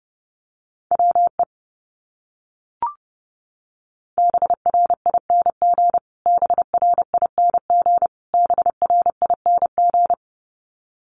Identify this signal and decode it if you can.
Morse code, heard as a keyed tone, about 30 wpm, 700 Hz. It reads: WE BRING BRING BRING
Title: Top 500 Words Repeated 30wpm